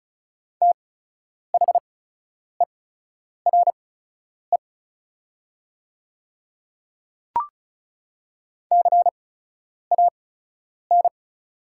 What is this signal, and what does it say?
Morse code, heard as a keyed tone, about 34 wpm, spaced as for 10 wpm, 690 Hz. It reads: THERE CAN
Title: 8x Character Spacing - Top 200 Words - 35wpm